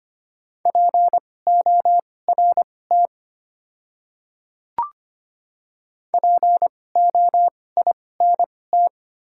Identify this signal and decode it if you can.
Morse code, heard as a keyed tone, about 25 wpm, spaced as for 18 wpm, 700 Hz. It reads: PORT POINT
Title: 2x Character Spacing - Top 500 Words - 25wpm